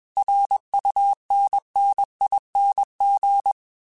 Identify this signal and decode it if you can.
Morse code, heard as a keyed tone, about 21 wpm, 800 Hz. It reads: RUNNING
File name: QSOs / QSOs-mp3s / kc7haa@20